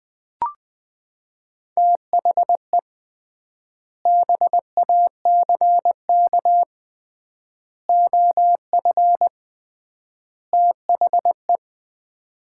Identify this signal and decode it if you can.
Morse code, heard as a keyed tone, about 19 wpm, 700 Hz. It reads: THE BACK OF THE
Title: 3x Word-Spacing - Sets of 4 Words - 20wpm